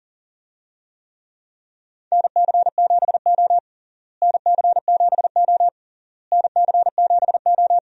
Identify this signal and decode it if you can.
Morse code, heard as a keyed tone, about 37 wpm, 700 Hz. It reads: NC7O NC7O NC7O
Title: USA Call Signs - 4 Characters - Repeated 3x - 40wpm